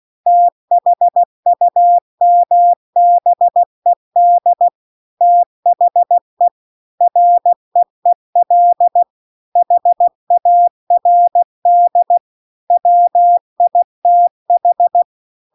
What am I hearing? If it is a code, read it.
Morse code, heard as a keyed tone, about 16 wpm, 700 Hz. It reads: THUMBED THE REEL HARD WITH
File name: CW-The-Big-Twoheaded-River-2-16WPM